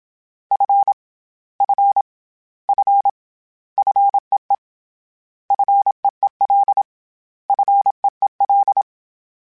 Morse code, heard as a keyed tone, about 27 wpm, 800 Hz. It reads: F F F FEE FEEL FEEL
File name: Short Runs / Short Runs-mp3s / SinglePrefixes @25